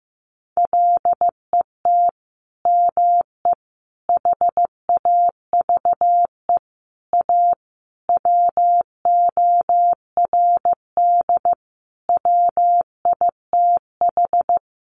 Morse code, heard as a keyed tone, about 15 wpm, 700 Hz. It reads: LET ME HAVE A WORD WITH
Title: Sentences from Top 900 Words 15wpm